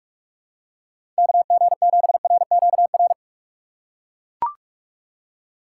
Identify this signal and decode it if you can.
Morse code, heard as a keyed tone, about 41 wpm, 700 Hz. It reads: KG7RQR
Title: USA Call Signs - 6 Characters - 4x Character Spacing - 45wpm